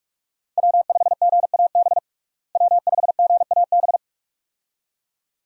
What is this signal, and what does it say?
Morse code, heard as a keyed tone, about 46 wpm, 700 Hz. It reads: W5GAB W5GAB
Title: USA Call Signs - 5 Characters - Repeated 2x - 45wpm